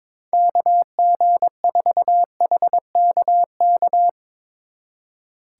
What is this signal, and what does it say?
Morse code, heard as a keyed tone, about 24 wpm, 700 Hz. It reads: KG4HKK
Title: Speed Racing - USA 2x3 Call Signs 22wpm